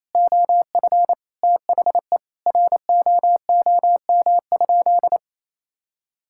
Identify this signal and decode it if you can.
Morse code, heard as a keyed tone, about 28 wpm, 700 Hz. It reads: OF THE ROOM?
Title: Sentences from Top 1000 Words - Review Part 6 - 28wpm